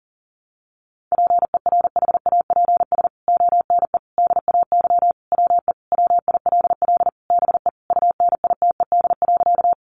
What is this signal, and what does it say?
Morse code, heard as a keyed tone, about 40 wpm, 700 Hz. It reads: PERHAPS ONE DAY WE WILL BE UNITED.
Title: Sentences from Top 600 Words 40wpm